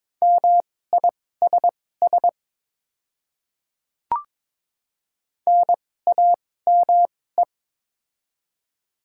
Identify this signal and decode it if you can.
Morse code, heard as a keyed tone, about 22 wpm, spaced as for 16 wpm, 700 Hz. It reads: MISS NAME
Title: 2x Character Spacing - Top 500 Words - 22wpm